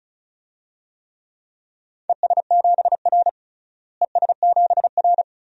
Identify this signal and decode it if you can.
Morse code, heard as a keyed tone, about 34 wpm, 700 Hz. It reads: ES7R ES7R
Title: DXCC-100 Call Signs - 4 Characters - Repeated 2x - 35wpm